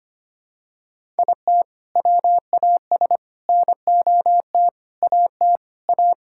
Morse code, heard as a keyed tone, about 25 wpm, 700 Hz. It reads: IT WAS NOT AT A
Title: Sentences from Top 1000 Words - Review Part 5 - 25wpm